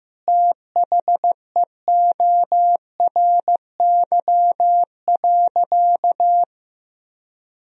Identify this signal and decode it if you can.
Morse code, heard as a keyed tone, about 15 wpm, 700 Hz. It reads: THEORY.
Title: Sentences from Top 1000 Words - Review Part 3 - 15wpm